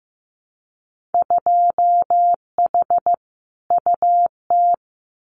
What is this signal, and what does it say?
Morse code, heard as a keyed tone, about 15 wpm, 700 Hz. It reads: 2H UT
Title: Winter Field Day Exchanges Encore 15wpm